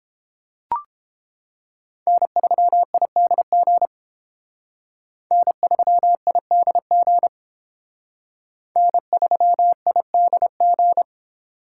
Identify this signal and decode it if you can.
Morse code, heard as a keyed tone, about 30 wpm, 700 Hz. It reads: N3IDG N3IDG N3IDG